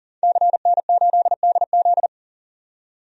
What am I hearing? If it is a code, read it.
Morse code, heard as a keyed tone, about 40 wpm, 700 Hz. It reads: CN8DZ
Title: DXCC-100 Call Signs - 5 Characters - 40wpm